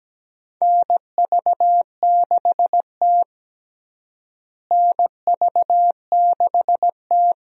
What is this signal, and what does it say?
Morse code, heard as a keyed tone, about 17 wpm, 700 Hz. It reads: NV6T NV6T
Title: USA Call Signs - 4 Characters - Repeated 2x - 17wpm